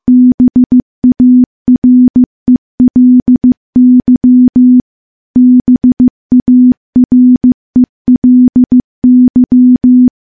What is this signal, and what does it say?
Morse code, heard as a keyed tone, about 15 wpm, 250 Hz. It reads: BARELY BARELY
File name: morse / morse-training-15WPM